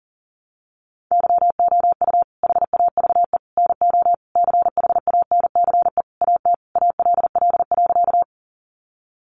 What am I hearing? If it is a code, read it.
Morse code, heard as a keyed tone, about 40 wpm, 700 Hz. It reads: YOU HAVE NO CHANCE AT ALL.
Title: Sentences from Top 800 Words 40wpm